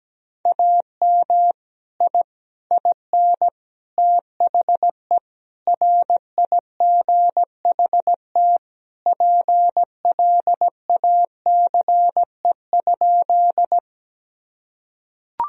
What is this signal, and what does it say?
Morse code, heard as a keyed tone, about 17 wpm, 700 Hz. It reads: AM I IN THE RIGHT PLACE?
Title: Sentences from Top 1000 Words - Review Part 2 - 17wpm